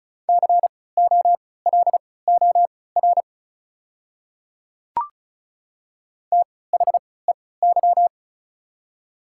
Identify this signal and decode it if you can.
Morse code, heard as a keyed tone, about 36 wpm, spaced as for 20 wpm, 700 Hz. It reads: COLOR THEY
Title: 3x Character Spacing - Top 300 Words - 35wpm